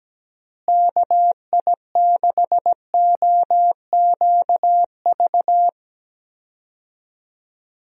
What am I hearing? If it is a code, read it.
Morse code, heard as a keyed tone, about 17 wpm, 700 Hz. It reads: KI6OQV
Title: USA Call Signs - 6 Characters - Repeated 2x - 17wpm